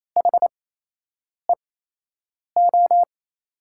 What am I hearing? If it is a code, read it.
Morse code, heard as a keyed tone, about 27 wpm, spaced as for 8 wpm, 700 Hz. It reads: HEO
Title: USA Call Signs - 6 Characters - 8x Character Spacing - 28wpm